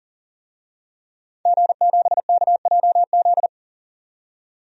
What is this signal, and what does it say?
Morse code, heard as a keyed tone, about 40 wpm, 700 Hz. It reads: G7KJZ
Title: DXCC-100 Call Signs - 5 Characters - 40wpm